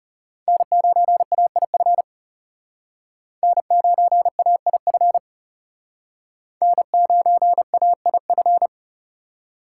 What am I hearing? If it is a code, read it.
Morse code, heard as a keyed tone, about 35 wpm, 700 Hz. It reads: N9AIF N9AIF N9AIF
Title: Speed Racing - USA 1x3 Call Signs 30wpm